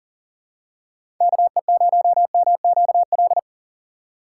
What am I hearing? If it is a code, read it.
Morse code, heard as a keyed tone, about 40 wpm, 700 Hz. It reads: KE0MQL
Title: USA Call Signs - 6 Characters - 3x Character Spacing - 40wpm